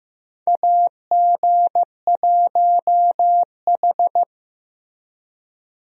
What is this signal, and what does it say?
Morse code, heard as a keyed tone, about 15 wpm, 700 Hz. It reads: AG1H
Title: USA Call Signs - 4 Characters - Repeated 2x - 15wpm